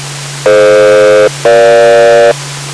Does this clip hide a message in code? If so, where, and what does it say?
Yes: aircraft SELCAL from 0.5 to 2.3 s, heard as two pairs of tones. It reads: DF-EG